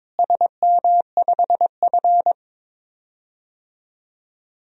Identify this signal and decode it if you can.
Morse code, heard as a keyed tone, about 22 wpm, 700 Hz. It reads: SM5F
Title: DXCC-100 Call Signs - 4 Characters - Repeated 3x - 22wpm